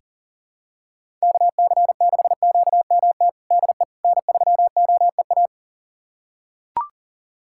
Morse code, heard as a keyed tone, about 40 wpm, 700 Hz. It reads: KC6QMT DE N3OEA
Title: USA Callsign DE Callsign 40wpm